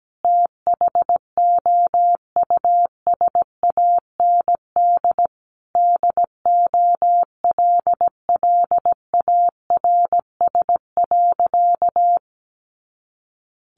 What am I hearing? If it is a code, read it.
Morse code, heard as a keyed tone, about 17 wpm, 700 Hz. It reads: THOUSAND DOLLARS.
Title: A Sound Of Thunder 17wpm